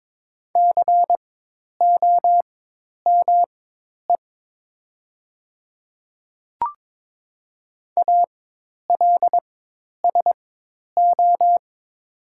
Morse code, heard as a keyed tone, about 22 wpm, spaced as for 10 wpm, 700 Hz. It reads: COME ALSO
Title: 4x Character Spacing - Top 200 Words - 22wpm